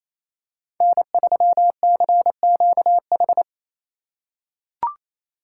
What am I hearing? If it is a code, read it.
Morse code, heard as a keyed tone, about 28 wpm, 700 Hz. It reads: N3CQH
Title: USA Call Signs - 5 Characters - 4x Character Spacing - 28wpm